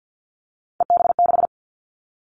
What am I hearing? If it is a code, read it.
Morse code, heard as a keyed tone, about 50 wpm, 700 Hz. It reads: EB6
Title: Callsign Prefixes for Top 50 DXCC - 50wpm